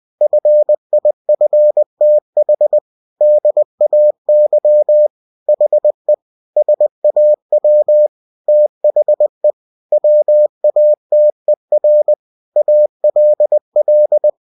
Morse code, heard as a keyed tone, about 20 wpm, 600 Hz. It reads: FIFTH DAY HE SAW THE WATER ALL
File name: cw-books / winnie-the-pooh / winnie-20wpm-page-0096